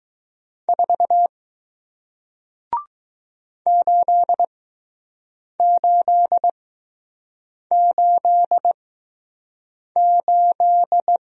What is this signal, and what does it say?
Morse code, heard as a keyed tone, about 20 wpm, spaced as for 6 wpm, 700 Hz. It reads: 4 8888